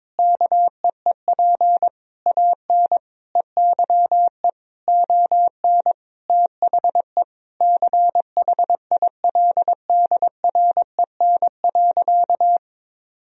Morse code, heard as a keyed tone, about 22 wpm, 700 Hz. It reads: KEEP AN EYE ON THE CHILDREN.
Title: Sentences from Top 300 Words and Abbreviations 22wpm